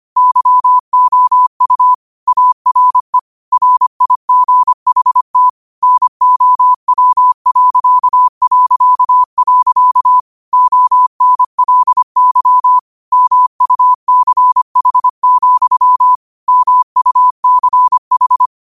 Morse code, heard as a keyed tone, about 25 wpm, 1000 Hz. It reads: YOU ARE RIGHT NOW... ONLY MUCH, MUCH
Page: Quotes for Thu, 14 Aug 2025 in Morse Code at 25 words per minute.